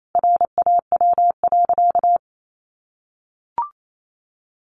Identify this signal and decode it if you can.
Morse code, heard as a keyed tone, about 28 wpm, 700 Hz. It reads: RAW.
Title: Sentences from Top 1500 Words 28wpm